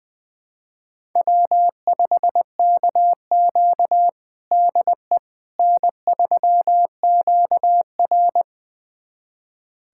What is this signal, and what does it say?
Morse code, heard as a keyed tone, about 21 wpm, 700 Hz. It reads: W5KQ DE N3QR
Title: USA Callsign DE Callsign - 4 Characters - 20wpm